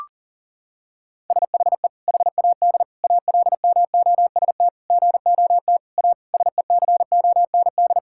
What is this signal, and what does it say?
Morse code, heard as a keyed tone, about 40 wpm, 700 Hz. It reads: SHE HAD ALMOST GOT A SECOND
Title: Sentences from Top 1000 Words - Review Part 5 - 40wpm